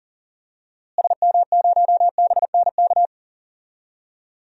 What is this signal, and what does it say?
Morse code, heard as a keyed tone, about 37 wpm, 700 Hz. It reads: SM0BNK
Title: DXCC-100 Call Signs - 6 Characters - Repeated 3x - 40wpm